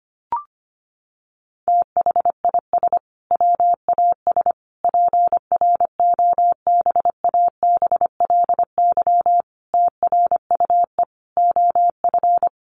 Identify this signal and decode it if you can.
Morse code, heard as a keyed tone, about 25 wpm, 700 Hz. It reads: THIS WAS PROBABLY TRUE OF
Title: Sentences from Top 700 Words 25wpm